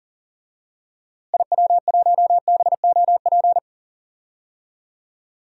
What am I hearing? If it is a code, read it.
Morse code, heard as a keyed tone, about 40 wpm, 700 Hz. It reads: IW1BOP
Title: DXCC-100 Call Signs - 6 Characters - Repeated 2x - 40wpm